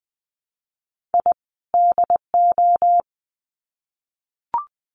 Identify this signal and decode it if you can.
Morse code, heard as a keyed tone, about 20 wpm, 700 Hz. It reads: I DO
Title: Sets of 2 Words 20wpm